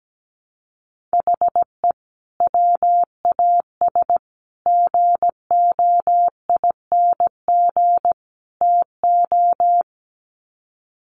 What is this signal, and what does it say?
Morse code, heard as a keyed tone, about 17 wpm, 700 Hz. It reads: HE WAS GOING TO
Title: Sets of 4 Words 17wpm